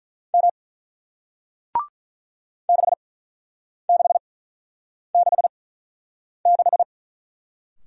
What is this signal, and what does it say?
Morse code, heard as a keyed tone, about 47 wpm, spaced as for 9 wpm, 700 Hz. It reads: M 6666